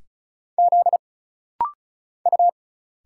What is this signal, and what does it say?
Morse code, heard as a keyed tone, about 36 wpm, 700 Hz. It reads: Z U
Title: Single Letters - Rapid-Fire 35wpm